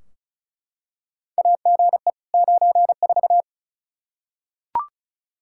Morse code, heard as a keyed tone, about 36 wpm, 700 Hz. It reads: AGE 94
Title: Practice QSO - Exchange 3 - More Info - 35wpm